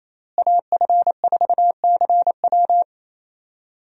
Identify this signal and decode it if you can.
Morse code, heard as a keyed tone, about 28 wpm, 700 Hz. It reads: AF4CW
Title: USA Call Signs - 5 Characters - Repeated 2x - 28wpm